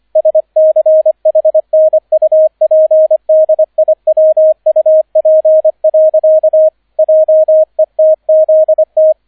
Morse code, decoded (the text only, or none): SCHNUPDIWUP. JETZT